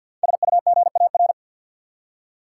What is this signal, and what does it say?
Morse code, heard as a keyed tone, about 53 wpm, 700 Hz. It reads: SUGAR
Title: Five Letter Words - 50wpm